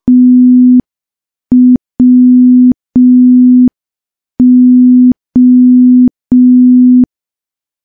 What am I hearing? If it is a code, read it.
Morse code, heard as a keyed tone, about 5 wpm, 250 Hz. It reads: TWO
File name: morse / morse-training-05WPM